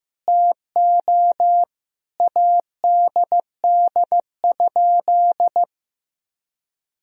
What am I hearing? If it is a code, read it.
Morse code, heard as a keyed tone, about 15 wpm, 700 Hz. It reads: TO ADD?
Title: Sentences from Top 1000 Words - Review Part 1 - 15wpm